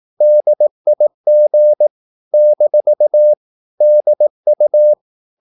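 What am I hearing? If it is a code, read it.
Morse code, heard as a keyed tone, about 18 wpm, 600 Hz. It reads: DIG - DU
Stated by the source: Slow (QRS) Morse practice at 18 wpm, CW med Gnister.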